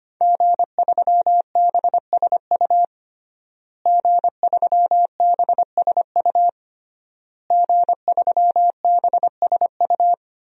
Morse code, heard as a keyed tone, about 25 wpm, 700 Hz. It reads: G3BSU G3BSU G3BSU
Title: DXCC-100 Call Signs - 5 Characters - Repeated 3x - 25wpm